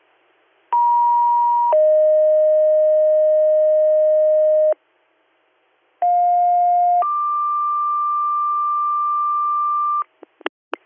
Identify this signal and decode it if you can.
Two-tone page, heard as a one-second tone, then a three-second tone, about 950 then 620 Hz, about 710 then 1180 Hz. It is heard from 0.7 to 10.0 s.